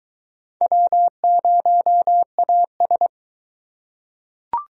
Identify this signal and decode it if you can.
Morse code, heard as a keyed tone, about 19 wpm, 700 Hz. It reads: W0AS